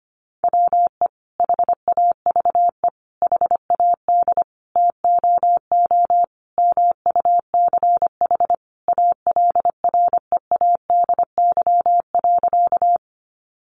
Morse code, heard as a keyed tone, about 25 wpm, 700 Hz. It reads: WE HAVE HAD TOO MUCH ALREADY.
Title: Sentences from Top 700 Words 25wpm